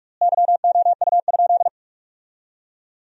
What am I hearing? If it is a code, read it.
Morse code, heard as a keyed tone, about 47 wpm, 700 Hz. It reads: YOU?
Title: 4x Word-Spacing - Sentences from Top 100 Words - 45wpm